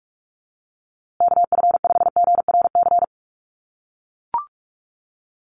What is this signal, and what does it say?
Morse code, heard as a keyed tone, about 44 wpm, 700 Hz. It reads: KF5GRC
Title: USA Call Signs Repeated 2x 45wpm